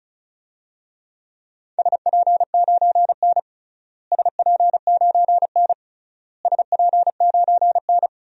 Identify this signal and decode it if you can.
Morse code, heard as a keyed tone, about 36 wpm, 700 Hz. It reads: SP9N SP9N SP9N
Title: DXCC-100 Call Signs - 4 Characters - Repeated 3x - 35wpm